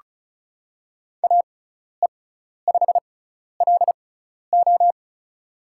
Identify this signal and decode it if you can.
Morse code, heard as a keyed tone, about 34 wpm, spaced as for 12 wpm, 700 Hz. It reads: AE5LO
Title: USA Call Signs - 6x Character Spacing - 35wpm